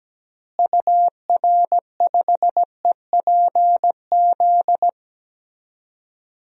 Morse code, heard as a keyed tone, about 17 wpm, 700 Hz. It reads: UR5EPZ
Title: DXCC-100 Call Signs - 6 Characters - Repeated 2x - 17wpm